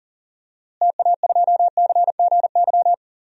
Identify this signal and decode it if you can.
Morse code, heard as a keyed tone, about 40 wpm, 700 Hz. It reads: TA2CGY
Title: DXCC-100 Call Signs - 6 Characters - Repeated 2x - 40wpm